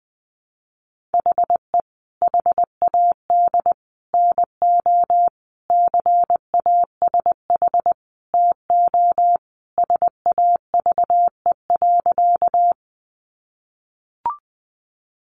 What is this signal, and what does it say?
Morse code, heard as a keyed tone, about 20 wpm, 700 Hz. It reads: HE HAD NO CASH TO SAVE.
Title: Sentences from Top 1100 Words 20wpm